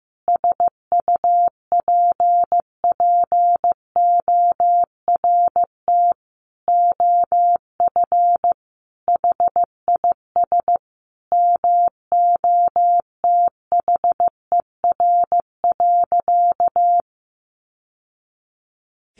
Morse code, heard as a keyed tone, about 14 wpm, 700 Hz. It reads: SUPPORT OF HIS MOTHER.
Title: Sentences from Top 800 Words 15wpm